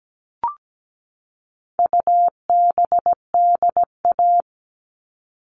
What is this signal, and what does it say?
Morse code, heard as a keyed tone, about 17 wpm, 700 Hz. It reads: UBDA
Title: All Cut Numbers - 4-Digit 17wpm